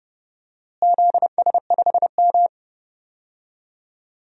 Morse code, heard as a keyed tone, about 30 wpm, 700 Hz. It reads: ZS5M
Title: DXCC-100 Call Signs - 4 Characters - Repeated 2x - 30wpm